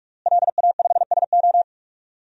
Morse code, heard as a keyed tone, about 46 wpm, 700 Hz. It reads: LA5SO
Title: DXCC-100 Call Signs - 5 Characters - Repeated 2x - 45wpm